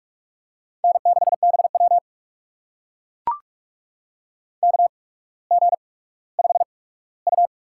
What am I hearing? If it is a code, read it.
Morse code, heard as a keyed tone, about 44 wpm, spaced as for 19 wpm, 690 Hz. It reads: N6BW KG5U
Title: USA Call Signs - 4 Characters - 8x Character Spacing - 45wpm